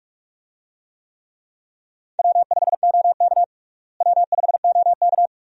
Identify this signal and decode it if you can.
Morse code, heard as a keyed tone, about 42 wpm, 700 Hz. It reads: W5OK W5OK
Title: USA Call Signs - 4 Characters - Repeated 2x - 45wpm